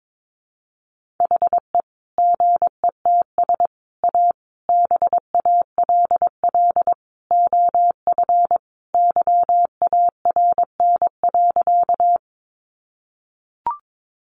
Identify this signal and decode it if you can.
Morse code, heard as a keyed tone, about 22 wpm, 700 Hz. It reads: HE GETS A BALL OF YARN.
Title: Additional Easy Phrases 22wpm